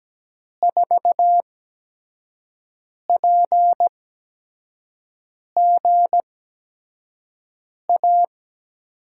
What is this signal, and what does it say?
Morse code, heard as a keyed tone, about 17 wpm, spaced as for 5 wpm, 700 Hz. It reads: 4PGA
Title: USA Call Signs - 6 Characters - 8x Character Spacing - 17wpm